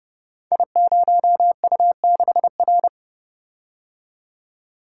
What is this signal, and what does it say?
Morse code, heard as a keyed tone, about 30 wpm, 700 Hz. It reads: I0U6R
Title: Code Groups - 5 Letter:Numbers - 30wpm